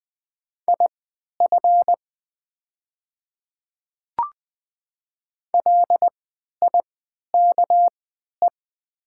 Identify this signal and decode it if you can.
Morse code, heard as a keyed tone, about 20 wpm, spaced as for 11 wpm, 700 Hz. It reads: IF LIKE
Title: 3x Character Spacing - Top 200 Words - 20wpm